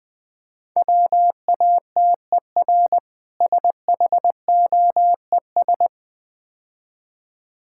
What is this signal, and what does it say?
Morse code, heard as a keyed tone, about 20 wpm, 700 Hz. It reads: WATER SHOES
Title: Difficult Words - Speed-Racing with Hints - 20wpm